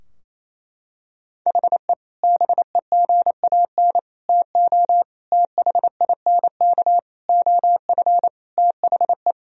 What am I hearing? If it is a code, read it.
Morse code, heard as a keyed tone, about 27 wpm, 700 Hz. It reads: HE BEGAN TO THINK OF THE
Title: Sentences from Top 1000 Words - Review Part 3 - 28wpm